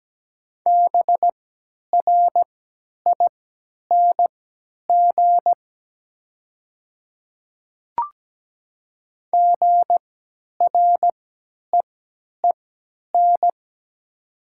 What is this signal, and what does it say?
Morse code, heard as a keyed tone, about 17 wpm, spaced as for 10 wpm, 700 Hz. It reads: BRING GREEN
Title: 3x Character Spacing - Top 500 Words - 17wpm